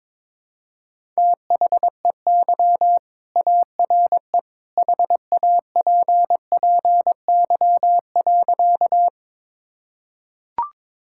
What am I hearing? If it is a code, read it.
Morse code, heard as a keyed tone, about 22 wpm, 700 Hz. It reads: THEY ARE HAPPY.
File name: practice / 22wpm / Sentences with 2-Word Phrases - 22wpm